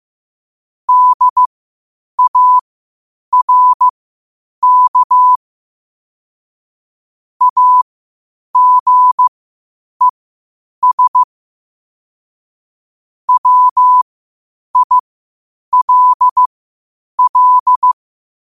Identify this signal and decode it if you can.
Morse code, heard as a keyed tone, about 15 wpm, spaced as for 8 wpm, 1000 Hz. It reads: DARK AGES WILL
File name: qotdfiles / 8wpm